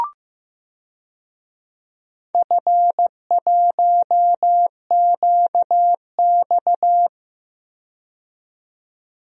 Morse code, heard as a keyed tone, about 15 wpm, 700 Hz. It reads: F1QX
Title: DXCC-100 Call Signs - 4 Characters - 15wpm